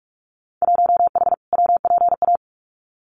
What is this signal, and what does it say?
Morse code, heard as a keyed tone, about 44 wpm, 700 Hz. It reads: 1H WPA
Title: Winter Field Day Exchanges 45wpm